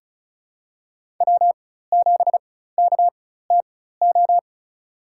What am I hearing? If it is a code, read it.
Morse code, heard as a keyed tone, about 34 wpm, spaced as for 16 wpm, 700 Hz. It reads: W7KTO
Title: USA Call Signs - 5 Characters - 4x Character Spacing - 35wpm